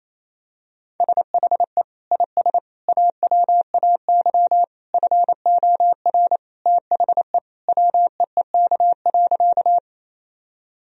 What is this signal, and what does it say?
Morse code, heard as a keyed tone, about 27 wpm, 700 Hz. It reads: SHE IS AWAY FOR THE WEEK.
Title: Sentences from Top 1000 Words - Review Part 2 - 28wpm